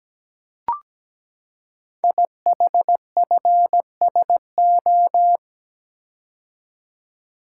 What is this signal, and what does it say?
Morse code, heard as a keyed tone, about 17 wpm, 700 Hz. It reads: IHFSO
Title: Code Groups - 5 Letters - 17wpm